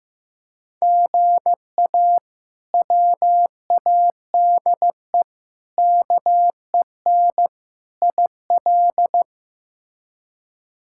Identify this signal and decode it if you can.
Morse code, heard as a keyed tone, about 15 wpm, 700 Hz. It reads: GA WADE KEN IL
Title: K1USN SST QSO Practice - 15wpm